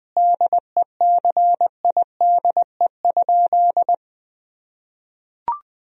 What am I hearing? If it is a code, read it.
Morse code, heard as a keyed tone, about 20 wpm, 700 Hz. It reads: DECIDE?
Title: Sentences from Top 1000 Words - Review Part 3 - 20wpm